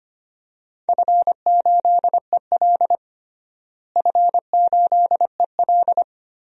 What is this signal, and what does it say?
Morse code, heard as a keyed tone, about 25 wpm, 700 Hz. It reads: F8EL F8EL
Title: DXCC-100 Call Signs - 4 Characters - Repeated 3x - 25wpm